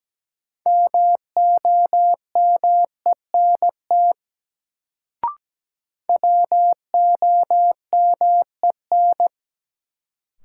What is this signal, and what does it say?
Morse code, heard as a keyed tone, about 17 wpm, 700 Hz. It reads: MOMENT WOMEN
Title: Sets of 3 Words - New Word Drill - 17wpm